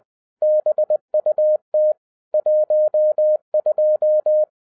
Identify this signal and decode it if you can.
Morse code, heard as a keyed tone, about 20 wpm, 610 Hz. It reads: BUT 12
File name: 2011-07-mp3 / n08 A LOT HAS CHANGED 20wpm